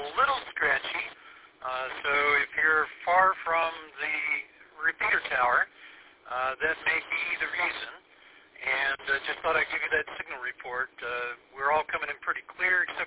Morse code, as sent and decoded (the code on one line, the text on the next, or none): none